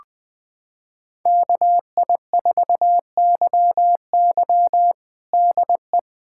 Morse code, heard as a keyed tone, about 21 wpm, 700 Hz. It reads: KI4YY DE
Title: USA Callsign DE Callsign - 5 Characters - 20wpm